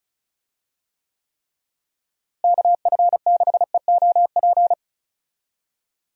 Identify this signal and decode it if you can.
Morse code, heard as a keyed tone, about 34 wpm, 700 Hz. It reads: KF6EOP
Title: USA Call Signs - 6 Characters - Repeated 3x - 35wpm